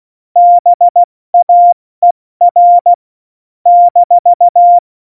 Morse code, heard as a keyed tone, about 16 wpm, 700 Hz. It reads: BAER -
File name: Morsekurs / Witze / Witz 17.Juli 2018, 70 F80